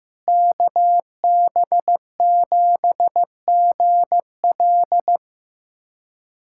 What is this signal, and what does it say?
Morse code, heard as a keyed tone, about 16 wpm, 700 Hz. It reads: KB7GL
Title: USA Callsign DE Callsign - 5 Characters - 15wpm